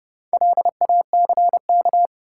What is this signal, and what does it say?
Morse code, heard as a keyed tone, about 30 wpm, 700 Hz. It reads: LACK
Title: Sentences from Top 1100 Words - New Word Drill Alt - 30wpm